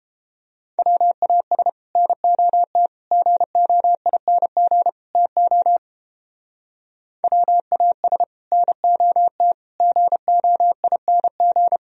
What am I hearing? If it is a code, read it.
Morse code, heard as a keyed tone, about 31 wpm, 700 Hz. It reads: WAS NOT GOING TO WAS NOT GOING